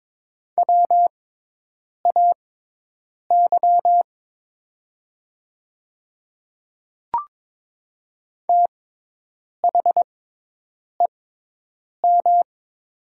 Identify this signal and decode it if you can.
Morse code, heard as a keyed tone, about 22 wpm, spaced as for 8 wpm, 700 Hz. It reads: WAY THEM
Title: 6x Character Spacing - Top 100 Words - 22wpm